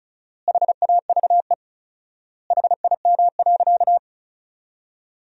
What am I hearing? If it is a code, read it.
Morse code, heard as a keyed tone, about 34 wpm, 700 Hz. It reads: HAVE HIM.
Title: 4x Word-Spacing - Sentences from Top 100 Words - 35wpm